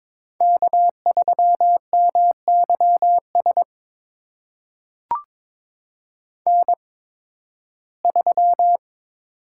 Morse code, heard as a keyed tone, about 22 wpm, spaced as for 14 wpm, 700 Hz. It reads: K3MYS N3
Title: USA Call Signs - 5 Characters - 8x Character Spacing - 22wpm